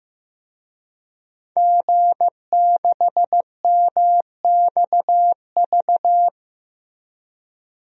Morse code, heard as a keyed tone, about 15 wpm, 700 Hz. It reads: G6MXV
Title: DXCC-100 Call Signs - 5 Characters - Repeated 3x - 15wpm